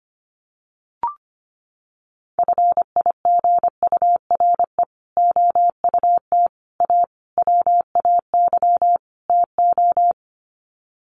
Morse code, heard as a keyed tone, about 25 wpm, 700 Hz. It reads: FIGURE OUT A WAY TO
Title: Sets of 5 Words Encore2 25wpm